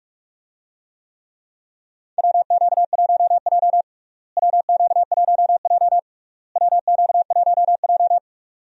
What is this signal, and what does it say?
Morse code, heard as a keyed tone, about 46 wpm, 700 Hz. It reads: WQ1J WQ1J WQ1J
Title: USA Call Signs - 4 Characters - Repeated 3x - 45wpm